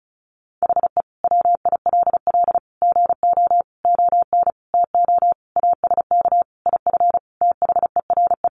Morse code, heard as a keyed tone, about 34 wpm, 700 Hz. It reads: HE WILL GO ON TO ASK IF THERE
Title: Sentences from Top 1500 Words 35wpm